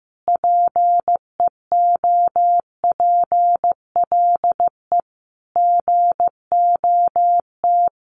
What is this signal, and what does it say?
Morse code, heard as a keyed tone, about 14 wpm, 700 Hz. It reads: PEOPLE GOT
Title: Sentences from Top 700 Words 15wpm